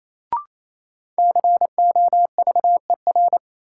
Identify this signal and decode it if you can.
Morse code, heard as a keyed tone, about 28 wpm, 700 Hz. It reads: COVER
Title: Sentences from Top 500 Words - New Word Drill - 28wpm